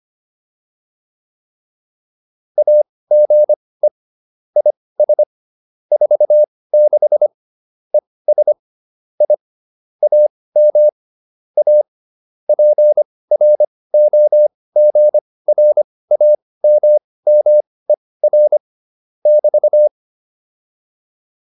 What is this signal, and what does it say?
Morse code, heard as a keyed tone, about 25 wpm, spaced as for 18 wpm, 600 Hz. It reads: AGE IS 46 ES I AM A PROGRAMMER <BT>
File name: QSO302 18